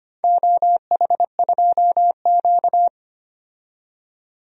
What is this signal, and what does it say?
Morse code, heard as a keyed tone, about 25 wpm, 700 Hz. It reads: OH2Q
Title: DXCC-100 Call Signs - 4 Characters - 25wpm